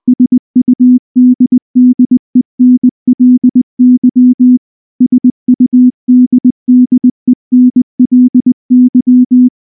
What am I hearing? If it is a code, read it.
Morse code, heard as a keyed tone, about 19 wpm, 250 Hz. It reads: SUDDENLY SUDDENLY
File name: morse / morse-training-20WPM-1